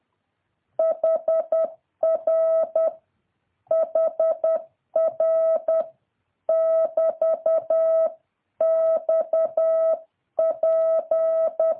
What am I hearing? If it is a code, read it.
Morse code, heard as a keyed tone, about 10 wpm, 650 Hz. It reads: HR HR <BT> XP
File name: jh8cbh / 2023cw